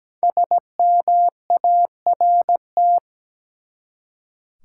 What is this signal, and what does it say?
Morse code, heard as a keyed tone, about 17 wpm, 700 Hz. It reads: SMART
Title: Five Letter Words - 17wpm